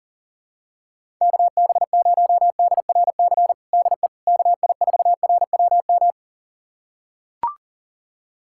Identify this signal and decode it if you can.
Morse code, heard as a keyed tone, about 40 wpm, 700 Hz. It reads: KB0DRC DE KI4RWM
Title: USA Callsign DE Callsign - 6 Characters - 40wpm